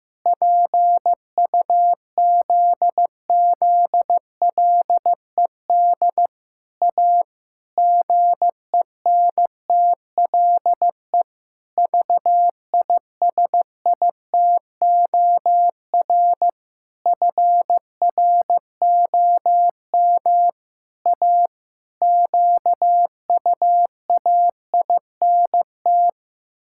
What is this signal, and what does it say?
Morse code, heard as a keyed tone, about 15 wpm, 700 Hz. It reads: PUZZLED A GENTLE VISITOR FROM A QUAINT
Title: Pangram Challenge v2 - 15wpm